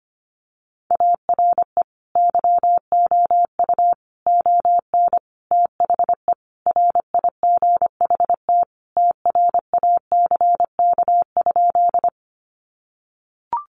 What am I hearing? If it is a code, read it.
Morse code, heard as a keyed tone, about 25 wpm, 700 Hz. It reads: ARE YOU ON THE RIGHT TRACK?
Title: Sentences from Top 1100 Words 25wpm